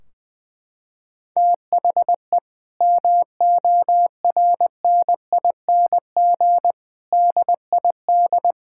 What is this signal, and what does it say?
Morse code, heard as a keyed tone, about 21 wpm, 700 Hz. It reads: THE MORNING DID
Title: Sentences from Top 1600 Words 20wpm